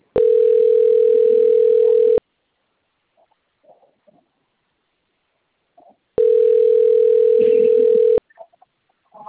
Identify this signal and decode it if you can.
Ringback tone (North America), 440 + 480 Hz.
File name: out- - - - - .96797